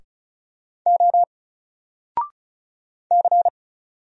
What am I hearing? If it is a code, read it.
Morse code, heard as a keyed tone, about 36 wpm, 700 Hz. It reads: O C
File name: Single-Letter-Number-35wpm